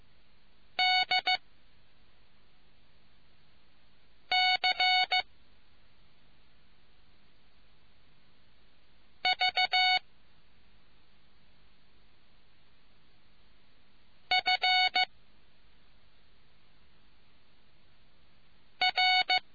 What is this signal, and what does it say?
Morse code, heard as a keyed tone, about 15 wpm, spaced as for 3 wpm, 2200 Hz. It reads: DC V FR